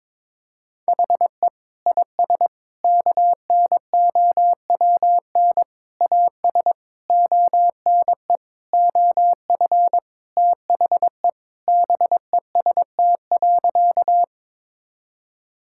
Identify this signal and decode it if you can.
Morse code, heard as a keyed tone, about 22 wpm, 700 Hz. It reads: HE IS KNOWN AS ONE OF THE BEST.
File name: practice / 22wpm / Sentences with 4-Word Phrases - 22wpm